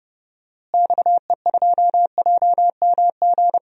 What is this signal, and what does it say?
Morse code, heard as a keyed tone, about 30 wpm, 700 Hz. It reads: XE2JMG
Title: DXCC-100 Call Signs - 6 Characters - Repeated 2x - 30wpm